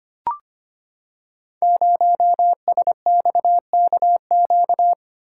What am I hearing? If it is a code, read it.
Morse code, heard as a keyed tone, about 25 wpm, 690 Hz. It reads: 0SXKQ
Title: Code Groups - 5 Letter:Numbers - 25wpm